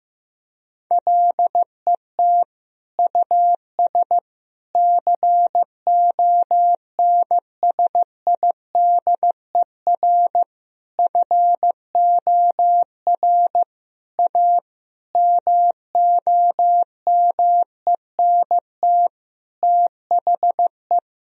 Morse code, heard as a keyed tone, about 15 wpm, 700 Hz. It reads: LET US CONSIDER FOR A MOMENT THE
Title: Sentences from Top 1000 Words - Review Part 4 - 15wpm